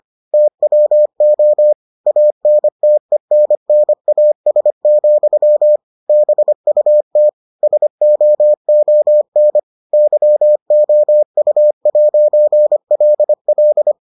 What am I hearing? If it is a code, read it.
Morse code, heard as a keyed tone, about 25 wpm, 600 Hz. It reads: TWO ANTENNAS, BUT SOON YOU'LL